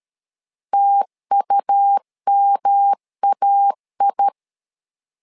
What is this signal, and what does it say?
Morse code, heard as a keyed tone, about 13 wpm, 800 Hz. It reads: TUMAI
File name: LEZIONI / 3 MWRO / 3 MW 12wpm 1633